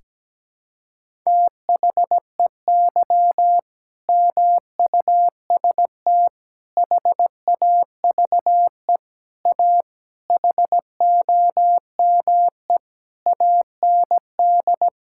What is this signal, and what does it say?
Morse code, heard as a keyed tone, about 17 wpm, 700 Hz. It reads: THEY MUST HAVE A HOME AND
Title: Sentences from Top 1000 Words - Review Part 5 - 17wpm